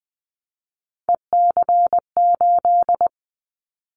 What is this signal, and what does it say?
Morse code, heard as a keyed tone, about 20 wpm, 700 Hz. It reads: EC8
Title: Callsign Prefixes for Top 100 DXCC - 20wpm